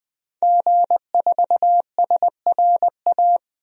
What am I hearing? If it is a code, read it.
Morse code, heard as a keyed tone, about 21 wpm, 700 Hz. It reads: G4SRA
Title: DXCC-100 Call Signs - 5 Characters - 20wpm